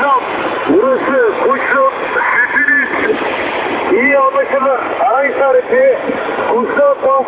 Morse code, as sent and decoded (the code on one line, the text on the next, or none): none